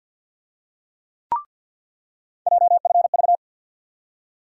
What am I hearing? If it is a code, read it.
Morse code, heard as a keyed tone, about 47 wpm, 700 Hz. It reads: JFV
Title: USA Call Sign Postfixes - 3 Letters 50wpm